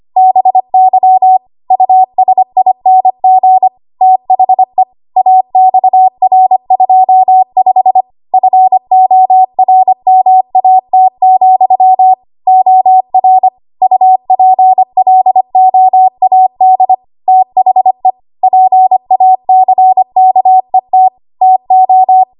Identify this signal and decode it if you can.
Morse code, heard as a keyed tone, about 25 wpm, 750 Hz. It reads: BY USING THE AXR25 FORMAT, OR UPLOAD THE PACKET TO
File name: Morse / 25WPM